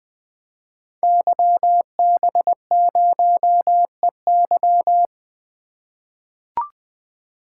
Morse code, heard as a keyed tone, about 20 wpm, 700 Hz. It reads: YB0EY
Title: Code Groups - 5 Letter:Numbers - 20wpm